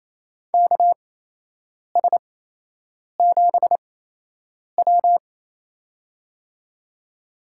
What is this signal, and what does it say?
Morse code, heard as a keyed tone, about 28 wpm, spaced as for 8 wpm, 700 Hz. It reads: KS7W
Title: USA Call Signs - 4 Characters - 8x Character Spacing - 28wpm